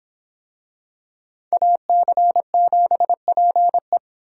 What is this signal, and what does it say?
Morse code, heard as a keyed tone, about 26 wpm, 700 Hz. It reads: AC7PE